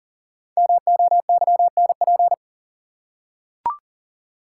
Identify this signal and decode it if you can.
Morse code, heard as a keyed tone, about 40 wpm, 700 Hz. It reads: MOYNP
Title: Code Groups - 5 Letter:Numbers - 40wpm